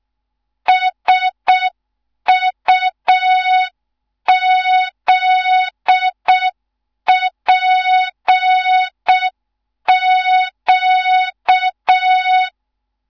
Morse code, heard as a keyed tone, about 6 wpm, 3700 Hz. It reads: SUZPQ
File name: Morze-HA2MN / hang / 15 GYAKORLAT